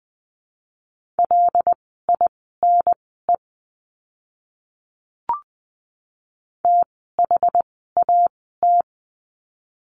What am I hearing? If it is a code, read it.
Morse code, heard as a keyed tone, about 20 wpm, spaced as for 14 wpm, 700 Hz. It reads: LINE THAT
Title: 115 Lesson 19f - Letter w - Words 20wpm